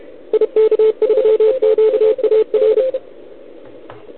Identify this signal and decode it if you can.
Morse code, heard as a keyed tone, about 33 wpm, 440 Hz. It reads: IK3QAR
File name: ARCHIV / OKDX2020 / OL4Wa